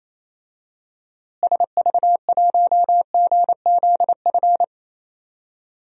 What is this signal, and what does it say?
Morse code, heard as a keyed tone, about 28 wpm, 700 Hz. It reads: SV1GZF
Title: DXCC-100 Call Signs - 6 Characters - Repeated 3x - 28wpm